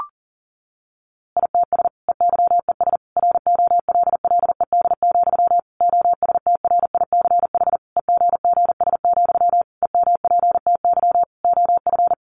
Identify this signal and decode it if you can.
Morse code, heard as a keyed tone, about 40 wpm, 700 Hz. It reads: ITS EYES ROLLED, OSTRICH EGGS, EMPTY OF
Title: A Sound Of Thunder 40wpm